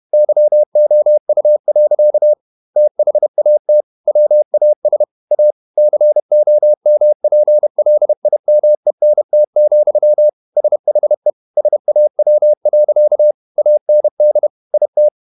Morse code, heard as a keyed tone, about 31 wpm, 600 Hz. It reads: YOU. THAT WAS A COMPLIMENT, SHE SAW. AND IT
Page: Never 31wpm | CW med Gnister